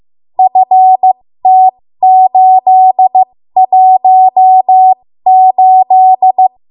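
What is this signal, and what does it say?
Morse code, heard as a keyed tone, about 15 wpm, spaced as for 13 wpm, 750 Hz. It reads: FT818
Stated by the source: 13 wpm Morse code practice archive transmission.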